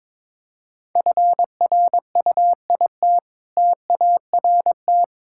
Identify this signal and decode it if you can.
Morse code, heard as a keyed tone, about 22 wpm, 690 Hz. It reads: FRUIT TART
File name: Difficult Words with Hints - 22wpm